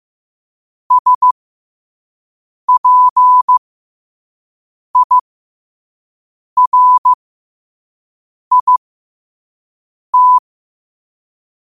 Morse code, heard as a keyed tone, about 15 wpm, spaced as for 5 wpm, 1000 Hz. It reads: SPIRIT
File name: qotdfiles / 5wpm